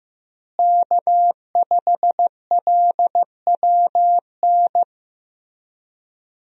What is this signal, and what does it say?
Morse code, heard as a keyed tone, about 15 wpm, 700 Hz. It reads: K5LWN
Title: USA Call Signs - 5 Characters - Repeated 3x - 15wpm